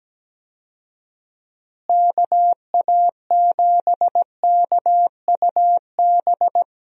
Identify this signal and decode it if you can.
Morse code, heard as a keyed tone, about 17 wpm, 700 Hz. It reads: KA7KUB
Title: USA Call Signs - 6 Characters - Repeated 2x - 17wpm